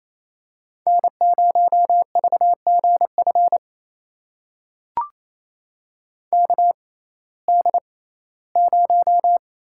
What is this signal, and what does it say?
Morse code, heard as a keyed tone, about 27 wpm, spaced as for 17 wpm, 700 Hz. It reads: N0VGF KD0
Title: USA Call Signs - 6x Character Spacing - 28wpm